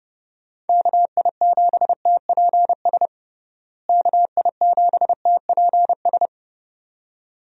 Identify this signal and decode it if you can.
Morse code, heard as a keyed tone, about 30 wpm, 700 Hz. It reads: KI7TPS KI7TPS
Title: USA Call Signs - 6 Characters - Repeated 2x - 30wpm